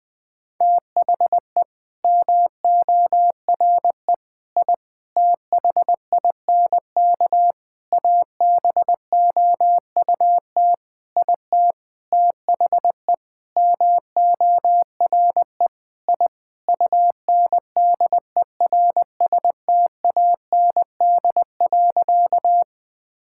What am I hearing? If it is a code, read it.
Morse code, heard as a keyed tone, about 20 wpm, 700 Hz. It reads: THE MORE I THINK ABOUT IT THE MORE I UNDERSTAND.
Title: Sentences with 9-Word Phrases - Spoken Beginning - 20wpm